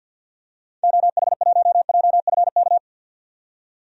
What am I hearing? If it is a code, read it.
Morse code, heard as a keyed tone, about 52 wpm, 700 Hz. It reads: OH1JFK
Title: DXCC-100 Call Signs - 6 Characters - 50wpm